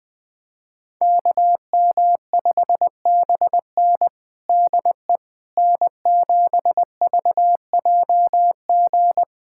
Morse code, heard as a keyed tone, about 20 wpm, 700 Hz. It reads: KM5BN DE N7VJG
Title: USA Callsign DE Callsign - 5 Characters - 20wpm